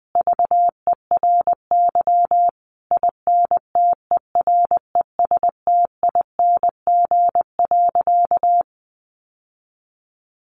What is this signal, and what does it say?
Morse code, heard as a keyed tone, about 20 wpm, 700 Hz. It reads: VERY INTERESTING.
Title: Sentences from Top 1200 Words 20wpm